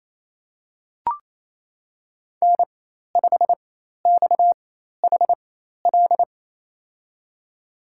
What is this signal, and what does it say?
Morse code, heard as a keyed tone, about 28 wpm, spaced as for 13 wpm, 700 Hz. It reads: N5XHL
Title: USA Call Signs - 5 Characters - 4x Character Spacing - 28wpm